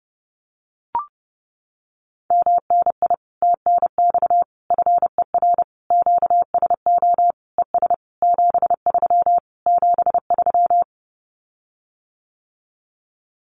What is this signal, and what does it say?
Morse code, heard as a keyed tone, about 30 wpm, 700 Hz. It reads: MNI TNX FER QSO ES 73 73
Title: Common QSO Phrases Encore 30wpm